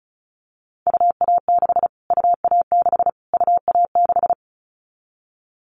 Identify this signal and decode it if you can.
Morse code, heard as a keyed tone, about 34 wpm, 700 Hz. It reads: UA6 UA6 UA6
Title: Callsign Prefixes for Top 25 DXCC - Repeated 3x - 35wpm